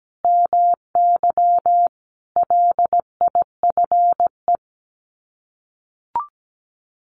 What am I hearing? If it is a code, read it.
Morse code, heard as a keyed tone, about 17 wpm, 700 Hz. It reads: MY LIFE
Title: Sets of 5 Words Encore2 17wpm